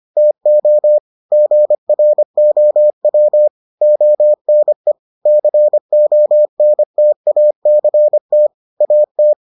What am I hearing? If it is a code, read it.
Morse code, heard as a keyed tone, about 25 wpm, 600 Hz. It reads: TO GROW ONE CONTACT AT